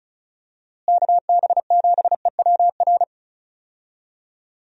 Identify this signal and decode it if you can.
Morse code, heard as a keyed tone, about 36 wpm, 700 Hz. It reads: KB7EWR
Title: USA Call Signs - 6 Characters - Repeated 2x - 35wpm